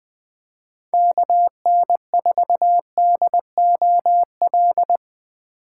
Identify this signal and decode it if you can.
Morse code, heard as a keyed tone, about 21 wpm, 700 Hz. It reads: KN4DOL
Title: USA Call Signs - 6 Characters - 4x Character Spacing - 20wpm